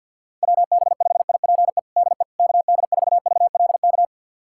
Morse code, heard as a keyed tone, about 50 wpm, 700 Hz. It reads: WB5IPE DE KD4VLK
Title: USA Callsign DE Callsign 50wpm